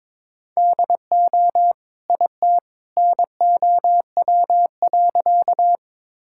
Morse code, heard as a keyed tone, about 22 wpm, 700 Hz. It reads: DO IT NOW.
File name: practice / 22wpm / Sentences with 2-Word Phrases - 22wpm